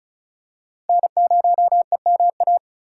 Morse code, heard as a keyed tone, about 34 wpm, 700 Hz. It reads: N0EMA
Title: USA Call Signs - 5 Characters - 4x Character Spacing - 35wpm